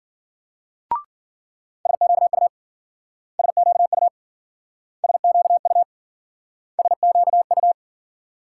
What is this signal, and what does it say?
Morse code, heard as a keyed tone, about 53 wpm, 700 Hz. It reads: SQU SQU SQU SQU